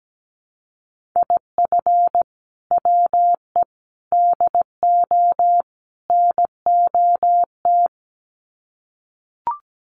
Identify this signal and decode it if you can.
Morse code, heard as a keyed tone, about 17 wpm, 700 Hz. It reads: IF WE DO NOT
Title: Sets of 4 Words 17wpm